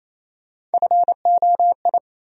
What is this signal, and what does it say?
Morse code, heard as a keyed tone, about 28 wpm, 700 Hz. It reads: FOI
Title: USA Call Sign Postfixes - 3 Letters 28wpm